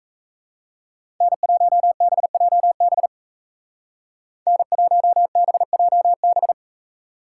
Morse code, heard as a keyed tone, about 35 wpm, 700 Hz. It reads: N1BJB N1BJB
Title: Speed Racing - USA 1x3 Call Signs 28wpm